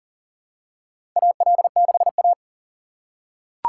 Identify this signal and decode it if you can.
Morse code, heard as a keyed tone, about 40 wpm, 700 Hz. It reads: AL6A
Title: USA Call Signs - 4x Character Spacing - 40wpm